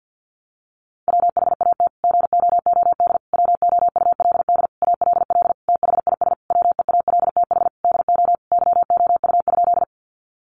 Morse code, heard as a keyed tone, about 50 wpm, 700 Hz. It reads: WHAT GOOD WOULD ALL THIS WEALTH DO YOU?
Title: Sentences from Top 1200 Words 50wpm